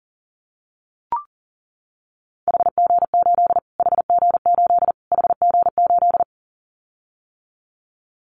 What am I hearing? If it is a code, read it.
Morse code, heard as a keyed tone, about 40 wpm, 700 Hz. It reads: HG8 HG8 HG8
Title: Callsign Prefixes for Top 25 DXCC - Repeated 3x - 40wpm